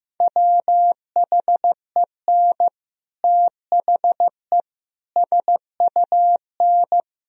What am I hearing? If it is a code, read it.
Morse code, heard as a keyed tone, about 16 wpm, 700 Hz. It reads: WHEN THE SUN
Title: Sentences from Top 1000 Words - Review Part 3 - 15wpm